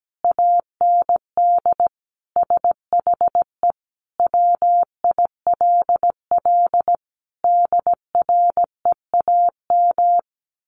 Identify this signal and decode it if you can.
Morse code, heard as a keyed tone, about 17 wpm, 700 Hz. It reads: AND SHE WILL DREAM
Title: Sentences from Top 1500 Words 17wpm